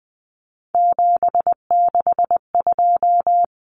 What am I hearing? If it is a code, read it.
Morse code, heard as a keyed tone, about 20 wpm, 700 Hz. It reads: 762
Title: Common Cut Numbers - 3-Digit 20wpm